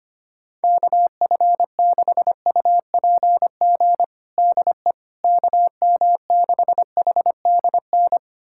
Morse code, heard as a keyed tone, about 25 wpm, 700 Hz. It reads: KF6UPG DE KM6HDN
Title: USA Callsign DE Callsign - 6 Characters - 25wpm